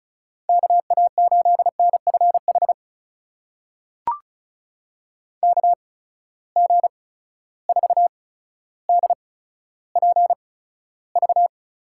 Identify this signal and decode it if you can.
Morse code, heard as a keyed tone, about 36 wpm, spaced as for 15 wpm, 700 Hz. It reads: KA8NFH KG4DPV
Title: USA Call Signs - 6 Characters - 8x Character Spacing - 35wpm